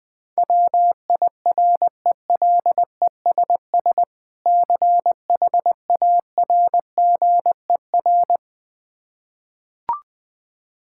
Morse code, heard as a keyed tone, about 20 wpm, 700 Hz. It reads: WIRELESS CHARGER
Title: Difficult Words with Hints - 20wpm